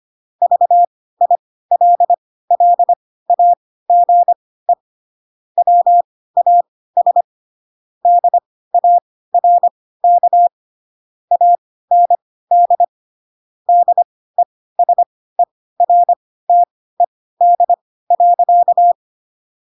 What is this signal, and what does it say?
Morse code, heard as a keyed tone, about 25 wpm, spaced as for 16 wpm, 700 Hz. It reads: VILLAGE WAS DARK AND DESERTED.